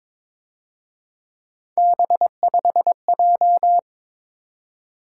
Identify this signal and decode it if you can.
Morse code, heard as a keyed tone, about 26 wpm, 700 Hz. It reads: B5J
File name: Speed-Racing - Three Letter-Number Combinations - 22wpm